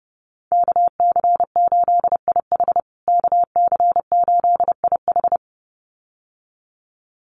Morse code, heard as a keyed tone, about 30 wpm, 700 Hz. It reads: KC8IH KC8IH
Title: USA Call Signs Repeated 2x 30wpm